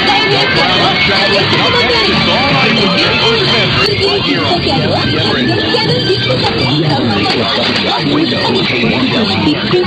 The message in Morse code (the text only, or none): none